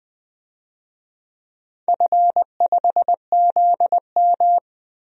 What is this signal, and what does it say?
Morse code, heard as a keyed tone, about 20 wpm, 700 Hz. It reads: F5ZM
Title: DXCC-100 Call Signs - 4 Characters - 20wpm